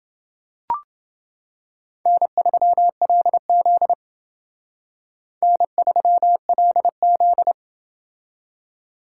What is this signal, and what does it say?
Morse code, heard as a keyed tone, about 29 wpm, 700 Hz. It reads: N3LZ N3LZ